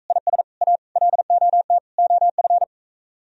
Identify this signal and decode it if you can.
Morse code, heard as a keyed tone, about 45 wpm, 700 Hz. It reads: IS A LOT OF